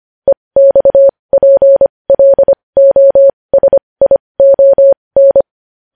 Morse code, heard as a keyed tone, about 25 wpm, spaced as for 20 wpm, 550 Hz. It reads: EXPLOSION
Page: News Headlines in Morse Code at 20 WPM